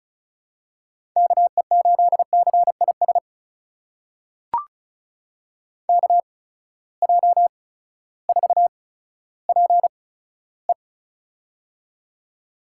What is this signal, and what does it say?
Morse code, heard as a keyed tone, about 36 wpm, spaced as for 16 wpm, 700 Hz. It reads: KE8CIS KJ4PE
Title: USA Call Signs - 8x Character Spacing - 35wpm